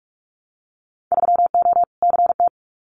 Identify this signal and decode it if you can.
Morse code, heard as a keyed tone, about 46 wpm, 700 Hz. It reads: 3O CT
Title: Winter Field Day Exchanges 45wpm